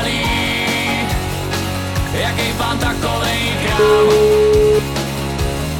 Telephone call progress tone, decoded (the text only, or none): ringback tone